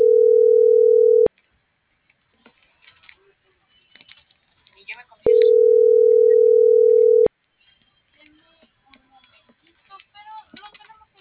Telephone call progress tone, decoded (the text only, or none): ringback tone